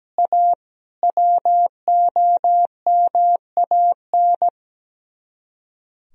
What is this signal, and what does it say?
Morse code, heard as a keyed tone, about 17 wpm, 700 Hz. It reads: A WOMAN
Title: Speed-Racing - Sets of 5 Words 17wpm